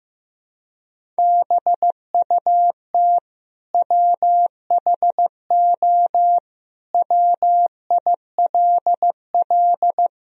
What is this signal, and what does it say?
Morse code, heard as a keyed tone, about 15 wpm, 700 Hz. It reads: BUT WHO WILL
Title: Sentences from Top 1000 Words - Review Part 2 - 15wpm